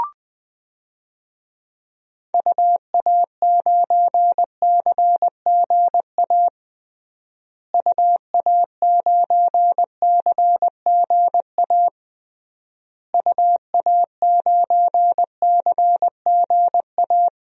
Morse code, heard as a keyed tone, about 20 wpm, 700 Hz. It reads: UA9CGA UA9CGA UA9CGA
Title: DXCC-100 Call Signs - 6 Characters - Repeated 3x - 20wpm